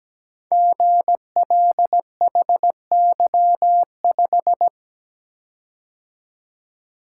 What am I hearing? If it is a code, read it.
Morse code, heard as a keyed tone, about 17 wpm, 700 Hz. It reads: GLHY5
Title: Code Groups - 5 Letter:Numbers - 17wpm